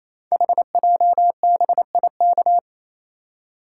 Morse code, heard as a keyed tone, about 27 wpm, 700 Hz. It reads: HJBIK
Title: Code Groups - 5 Letters - 28wpm